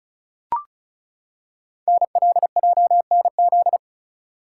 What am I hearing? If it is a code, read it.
Morse code, heard as a keyed tone, about 36 wpm, 690 Hz. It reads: NLJNZ
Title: Code Groups - 5 Letters - 35wpm